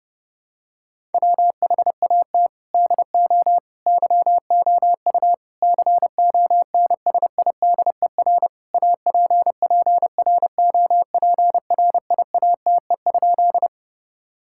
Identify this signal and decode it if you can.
Morse code, heard as a keyed tone, about 30 wpm, 700 Hz. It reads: WHAT DO YOU CONSIDER APPROPRIATE?
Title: Sentences from Top 1000 Words - Review Part 5 - 30wpm